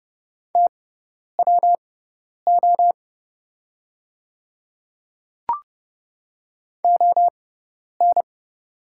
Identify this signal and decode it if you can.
Morse code, heard as a keyed tone, about 29 wpm, spaced as for 10 wpm, 690 Hz. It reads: TWO ON
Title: 6x Character Spacing - Top 100 Words - 30wpm